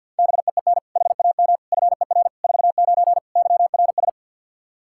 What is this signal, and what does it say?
Morse code, heard as a keyed tone, about 46 wpm, 700 Hz. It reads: BEEN HAM FER 49 YRS
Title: Practice QSO - Exchange 3 - More Info - 50wpm